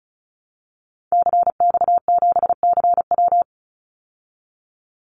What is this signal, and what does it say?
Morse code, heard as a keyed tone, about 34 wpm, 700 Hz. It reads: CX7CW
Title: Call Signs Repeated 3x - 35wpm